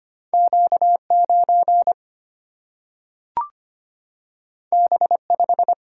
Morse code, heard as a keyed tone, about 25 wpm, 700 Hz. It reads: Q 9 B 5
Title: Single-Letter Single-Number 25wpm